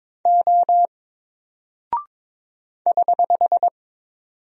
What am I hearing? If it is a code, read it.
Morse code, heard as a keyed tone, about 22 wpm, 700 Hz. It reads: O <HH>
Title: Single Character (Letter:Number:Punctuation:Prosign) Alt 22wpm